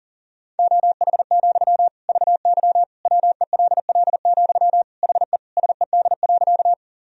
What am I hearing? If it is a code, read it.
Morse code, heard as a keyed tone, about 40 wpm, 700 Hz. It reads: OH, VY WELL, HE SED.
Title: Sentences from Top 500 Words and Abbreviations 40wpm